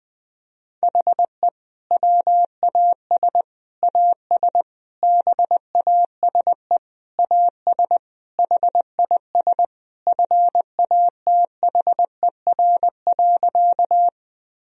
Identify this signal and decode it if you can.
Morse code, heard as a keyed tone, about 20 wpm, 700 Hz. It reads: HE WAS AS BASE AS HIS FATHER.
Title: Sentences from Top 1000 Words - Review Part 1 - 20wpm